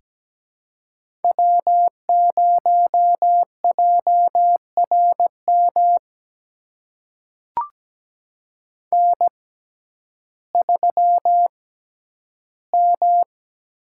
Morse code, heard as a keyed tone, about 17 wpm, spaced as for 10 wpm, 700 Hz. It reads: W0JRM N3M
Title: USA Call Signs - 5 Characters - 6x Character Spacing - 17wpm